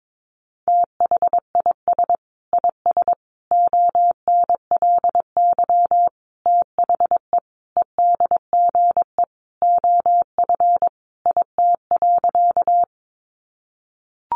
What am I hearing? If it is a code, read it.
Morse code, heard as a keyed tone, about 22 wpm, 700 Hz. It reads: THIS IS ONLY THE EDGE OF IT.
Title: Sentences from Top 1300 Words 22wpm